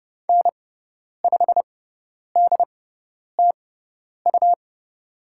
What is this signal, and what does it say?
Morse code, heard as a keyed tone, about 30 wpm, spaced as for 10 wpm, 700 Hz. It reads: N5DTU
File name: Lesson 17g - Letter u - Call Signs 30wpm